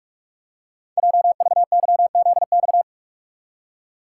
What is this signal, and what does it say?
Morse code, heard as a keyed tone, about 44 wpm, 700 Hz. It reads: JVYZX
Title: Code Groups - 5 Letters - 45wpm